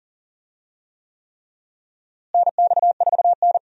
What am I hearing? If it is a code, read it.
Morse code, heard as a keyed tone, about 37 wpm, 700 Hz. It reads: NX4N
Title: USA Call Signs - 4 Characters - Repeated 3x - 40wpm